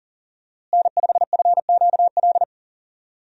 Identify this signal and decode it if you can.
Morse code, heard as a keyed tone, about 37 wpm, 700 Hz. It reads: N5FQL
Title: USA Call Signs - 5 Characters - Repeated 2x - 40wpm